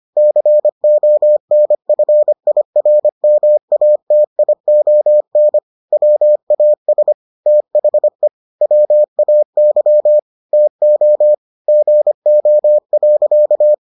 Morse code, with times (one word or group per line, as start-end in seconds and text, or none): 0.2-5.6 CONFIRMATION
5.9-7.1 WAS
7.5-8.3 THE
8.6-10.2 WAY
10.5-11.3 TO
11.7-13.7 GO.